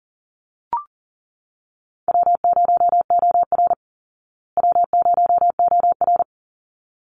Morse code, heard as a keyed tone, about 40 wpm, 700 Hz. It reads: W0OR W0OR
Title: Call Signs Repeated 3x - 40wpm